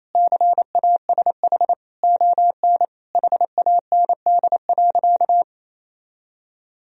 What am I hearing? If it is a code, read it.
Morse code, heard as a keyed tone, about 28 wpm, 700 Hz. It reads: CASH ON HAND.
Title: Sentences from Top 1000 Words - Review Part 4 - 28wpm